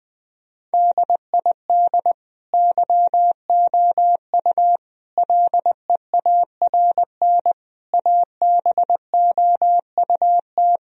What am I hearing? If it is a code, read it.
Morse code, heard as a keyed tone, about 20 wpm, 700 Hz. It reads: DID YOU LEARN ABOUT
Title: Sentences from Top 1000 Words - Review Part 4 - 20wpm